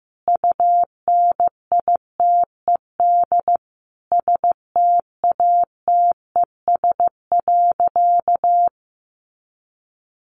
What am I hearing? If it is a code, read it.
Morse code, heard as a keyed tone, about 15 wpm, 700 Hz. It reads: UNITED STATES.
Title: A Sound Of Thunder 15wpm